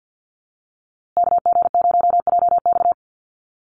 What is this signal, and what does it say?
Morse code, heard as a keyed tone, about 48 wpm, 700 Hz. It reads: KG0JX
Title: USA Call Signs Repeated 3x 50wpm